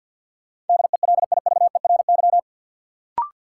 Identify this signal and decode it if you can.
Morse code, heard as a keyed tone, about 53 wpm, 700 Hz. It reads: DELIVERY
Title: Sentences from Top 1500 Words - New Word Drill - 50wpm